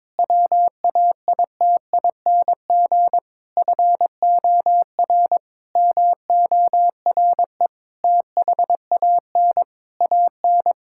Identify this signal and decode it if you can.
Morse code, heard as a keyed tone, about 21 wpm, 700 Hz. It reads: WAITING FOR MORE THAN AN
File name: practice / 22wpm / Sentences with 3-Word Phrases - 22wpm